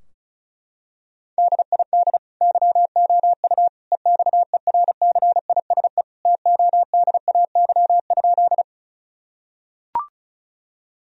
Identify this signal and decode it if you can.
Morse code, heard as a keyed tone, about 34 wpm, 700 Hz. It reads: DID YOU EXERCISE TODAY?
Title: Sentences from Top 1000 Words - Review Part 4 - 35wpm